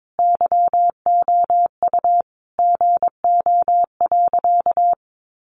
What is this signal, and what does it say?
Morse code, heard as a keyed tone, about 22 wpm, 700 Hz. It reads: YOU GO.
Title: Sentences from Top 600 Words 22wpm